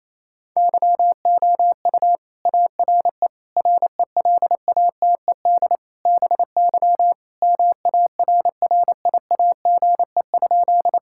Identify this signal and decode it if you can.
Morse code, heard as a keyed tone, about 27 wpm, 700 Hz. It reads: YOU ARE RELATED BY MARRIAGE?
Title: Sentences from Top 1000 Words - Review Part 5 - 28wpm